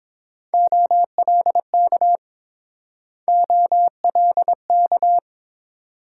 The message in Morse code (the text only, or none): OLK OLK